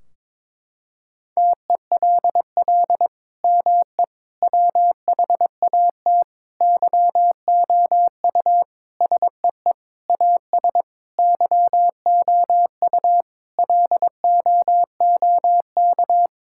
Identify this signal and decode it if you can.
Morse code, heard as a keyed tone, about 21 wpm, 700 Hz. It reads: TELL ME WHAT YOU SEE AS YOU LOOK
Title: Sentences from Top 1600 Words 22wpm